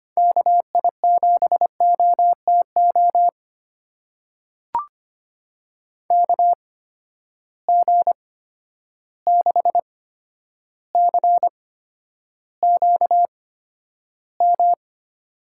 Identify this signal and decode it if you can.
Morse code, heard as a keyed tone, about 25 wpm, spaced as for 11 wpm, 700 Hz. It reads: KI7OTO KG6CQM
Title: USA Call Signs - 6 Characters - 8x Character Spacing - 25wpm